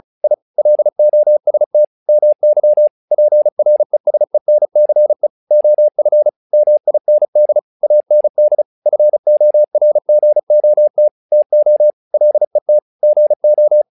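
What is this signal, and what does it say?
Morse code, heard as a keyed tone, about 36 wpm, 600 Hz. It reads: I LOST MY PRESENCE OF MIND AND FORGOT TO LET GO
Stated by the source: Morse practice text "Tower Trouble" sent at 35 WPM, from the SKCC - Straight Key Century Club learning center audio files.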